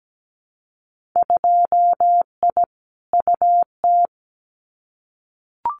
Morse code, heard as a keyed tone, about 17 wpm, 700 Hz. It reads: 2I UT
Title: Winter Field Day Exchanges 17wpm